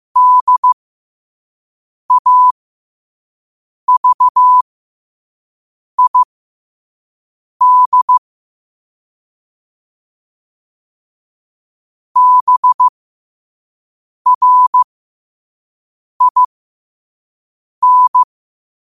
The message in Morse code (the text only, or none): DAVID BRIN